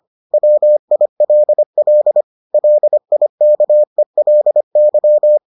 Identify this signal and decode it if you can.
Morse code, heard as a keyed tone, about 25 wpm, 600 Hz. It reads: WILL LIKELY